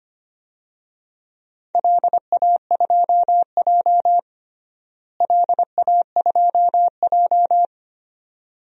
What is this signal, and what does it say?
Morse code, heard as a keyed tone, about 25 wpm, 700 Hz. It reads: LA2J LA2J
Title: DXCC-100 Call Signs - 4 Characters - Repeated 2x - 25wpm